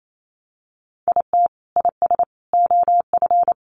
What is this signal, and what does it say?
Morse code, heard as a keyed tone, about 28 wpm, 700 Hz. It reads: IT IS OF
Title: Sentences from Top 1200 Words 28wpm